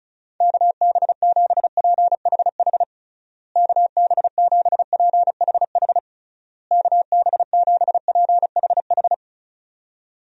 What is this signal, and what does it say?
Morse code, heard as a keyed tone, about 36 wpm, 700 Hz. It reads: KB7PHH KB7PHH KB7PHH
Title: USA Call Signs - 6 Characters - Repeated 3x - 35wpm